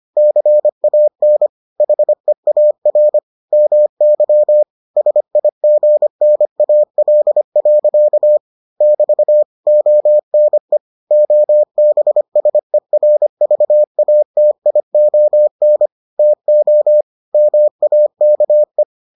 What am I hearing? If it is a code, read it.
Morse code, heard as a keyed tone, about 25 wpm, 600 Hz. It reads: CAN HEAR MY SIGNAL. <BT> ONE OBSERVATION TO MAKE